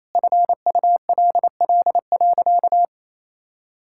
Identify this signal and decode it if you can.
Morse code, heard as a keyed tone, about 28 wpm, 700 Hz. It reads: FULL.
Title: Sentences from Top 1600 Words 28wpm